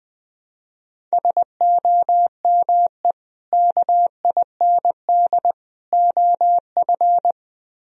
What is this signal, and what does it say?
Morse code, heard as a keyed tone, about 22 wpm, 700 Hz. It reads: SOME KIND OF
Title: Speed-Racing - Sets of 3 Words 20wpm